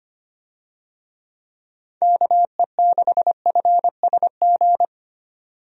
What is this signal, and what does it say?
Morse code, heard as a keyed tone, about 25 wpm, 700 Hz. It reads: KE6FSG
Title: USA Call Signs - 6 Characters - 25wpm